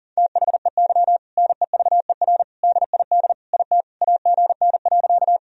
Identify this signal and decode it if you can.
Morse code, heard as a keyed tone, about 40 wpm, 700 Hz. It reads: THEY NEVER DID IT AGN.
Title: Sentences from Top 300 Words and Abbreviations 40wpm